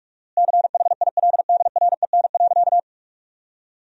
Morse code, heard as a keyed tone, about 45 wpm, 700 Hz. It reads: CHILDREN.
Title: 4x Word-Spacing - Sentences from Top 300 Words - 45wpm